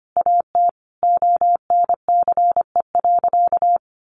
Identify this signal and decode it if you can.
Morse code, heard as a keyed tone, about 25 wpm, 700 Hz. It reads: AT ONCE.
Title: Sentences from Top 1100 Words 25wpm